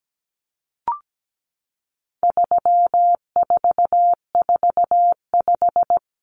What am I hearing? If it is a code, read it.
Morse code, heard as a keyed tone, about 17 wpm, 700 Hz. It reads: 3445
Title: Numbers - 4-Digit 17wpm